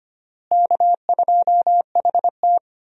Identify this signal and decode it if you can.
Morse code, heard as a keyed tone, about 25 wpm, 700 Hz. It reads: K2HT
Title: USA Call Signs - 4 Characters - Repeated 3x - 25wpm